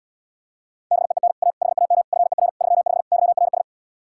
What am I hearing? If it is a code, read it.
Morse code, heard as a keyed tone, about 78 wpm, 700 Hz. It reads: BEEN A HAM FER 35 YRS